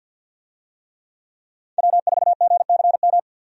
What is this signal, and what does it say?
Morse code, heard as a keyed tone, about 50 wpm, 700 Hz. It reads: W4GCM
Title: USA Call Signs - 5 Characters - 50wpm